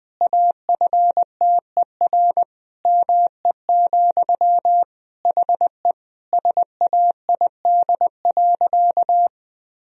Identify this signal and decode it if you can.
Morse code, heard as a keyed tone, about 20 wpm, 700 Hz. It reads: AFTER ME, HE SAID.
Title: Sentences from Top 1000 Words - Review Part 1 - 20wpm